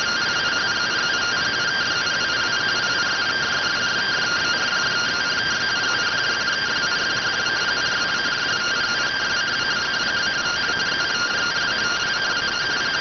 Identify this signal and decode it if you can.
Two-tone teleprinter signal, about 170 Hz shift.